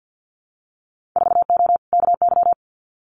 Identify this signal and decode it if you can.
Morse code, heard as a keyed tone, about 50 wpm, 700 Hz. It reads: 4O KY
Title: Winter Field Day Exchanges Encore 50wpm